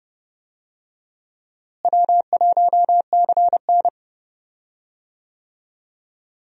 Morse code, heard as a keyed tone, about 31 wpm, 700 Hz. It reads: W1CN
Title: USA Call Signs - 4 Characters - Repeated 3x - 30wpm